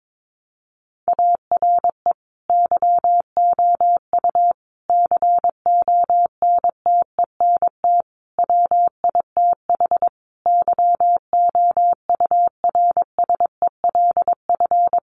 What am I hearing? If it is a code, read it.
Morse code, heard as a keyed tone, about 22 wpm, 700 Hz. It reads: ARE YOU CONTENT WITH YOURSELF
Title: Sentences from Top 900 Words 22wpm